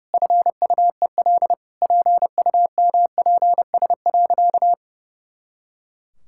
Morse code, heard as a keyed tone, about 28 wpm, 700 Hz. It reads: FUEL PUMPS.
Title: Pangram Challenge v2 - 30wpm